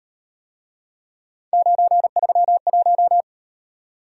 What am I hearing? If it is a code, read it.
Morse code, heard as a keyed tone, about 38 wpm, 700 Hz. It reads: 931